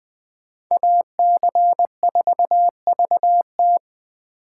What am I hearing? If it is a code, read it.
Morse code, heard as a keyed tone, about 21 wpm, 700 Hz. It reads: AC4VT
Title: Speed Racing - USA 2x2 Call Signs 17wpm